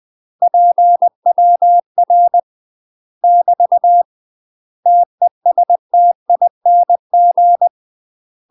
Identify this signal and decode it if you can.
Morse code, heard as a keyed tone, about 20 wpm, 700 Hz. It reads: PWR <BT> TESTING
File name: MASTER QSOs TRAINING/20WPM